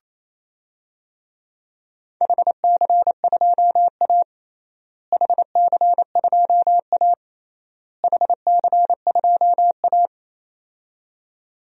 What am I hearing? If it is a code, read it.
Morse code, heard as a keyed tone, about 28 wpm, 700 Hz. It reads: HC2A HC2A HC2A
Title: DXCC-100 Call Signs - 4 Characters - Repeated 3x - 28wpm